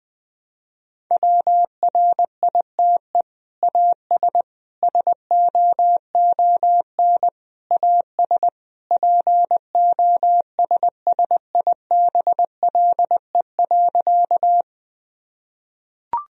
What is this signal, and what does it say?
Morse code, heard as a keyed tone, about 20 wpm, 700 Hz. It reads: WRITE AS SOON AS POSSIBLE.
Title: Sentences from Top 1000 Words - Review Part 1 - 20wpm